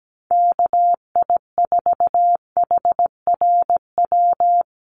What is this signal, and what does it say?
Morse code, heard as a keyed tone, about 17 wpm, 700 Hz. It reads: KI4HRW
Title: USA Call Signs Repeated 3x 17wpm